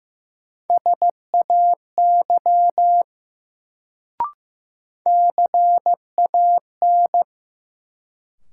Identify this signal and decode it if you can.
Morse code, heard as a keyed tone, about 15 wpm, 700 Hz. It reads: SAY CAN
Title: Sets of 2 Words - New Word Drill - 15wpm